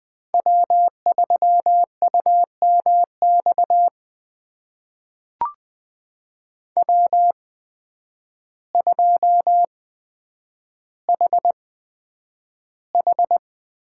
Morse code, heard as a keyed tone, about 20 wpm, spaced as for 9 wpm, 700 Hz. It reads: W3UMX W2HH
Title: USA Call Signs - 5 Characters - 8x Character Spacing - 20wpm